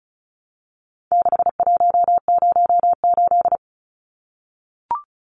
Morse code, heard as a keyed tone, about 34 wpm, 700 Hz. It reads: 6108
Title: Numbers - 4-Digit 35wpm